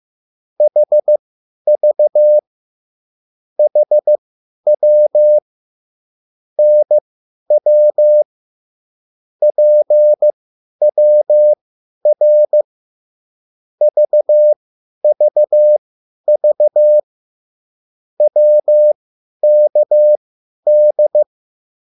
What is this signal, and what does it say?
Morse code, heard as a keyed tone, about 15 wpm, spaced as for 10 wpm, 600 Hz. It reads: HV HW NW PWR VVV WKD